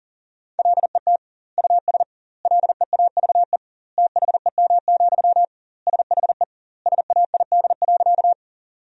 Morse code, heard as a keyed tone, about 40 wpm, 700 Hz. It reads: LET US LEAVE THEM, SHE SAID.
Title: 2x Word-Spacing - Sentences from Top 400 Words - 40wpm